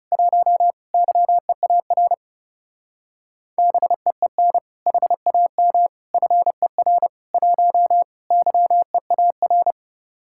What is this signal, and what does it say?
Morse code, heard as a keyed tone, about 38 wpm, 700 Hz. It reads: 1 YEAR BEEN HAM FER 1 YEAR